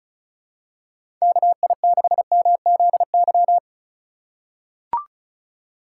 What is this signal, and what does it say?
Morse code, heard as a keyed tone, about 36 wpm, 700 Hz. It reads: KI6MZY
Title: USA Call Signs - 2x Character Spacing - 35wpm